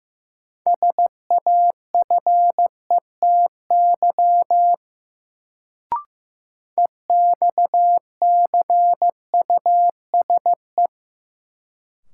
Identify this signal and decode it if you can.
Morse code, heard as a keyed tone, about 16 wpm, 700 Hz. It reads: SAFETY EXCUSE
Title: Sentences from Top 700 Words - New Word Drill - 15wpm